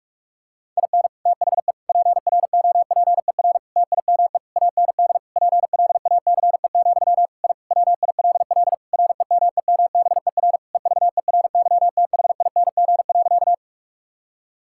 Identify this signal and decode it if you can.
Morse code, heard as a keyed tone, about 46 wpm, 700 Hz. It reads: IN THE PROPER TIME AND PLACE, I WILL REMEMBER EVERYTHING.
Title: Sentences from Top 1000 Words - Review Part 5 - 45wpm